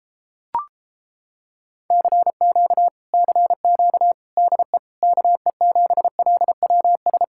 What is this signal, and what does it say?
Morse code, heard as a keyed tone, about 29 wpm, 700 Hz. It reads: CQ CQ DE KE7LWS